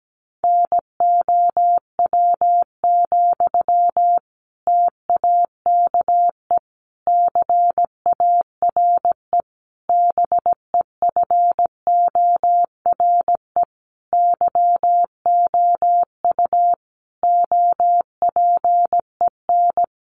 Morse code, heard as a keyed tone, about 17 wpm, 700 Hz. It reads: NOW, TAKE CARE BEFORE YOU OPEN
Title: Sentences from Top 1500 Words 17wpm